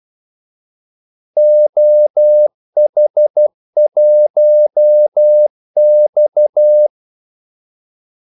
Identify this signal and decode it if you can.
Morse code, heard as a keyed tone, about 12 wpm, 600 Hz. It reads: OH1X
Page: 2 ekstra ordmellemrum mellem kaldesignalerne